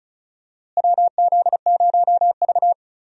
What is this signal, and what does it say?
Morse code, heard as a keyed tone, about 35 wpm, 700 Hz. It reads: WZ0V
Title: USA Call Signs - 4 Characters - Repeated 3x - 35wpm